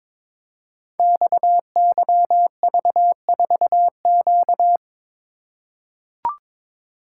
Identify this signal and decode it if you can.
Morse code, heard as a keyed tone, about 23 wpm, 700 Hz. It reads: XYV4Q
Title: Code Groups - 5 Letter:Numbers - 22wpm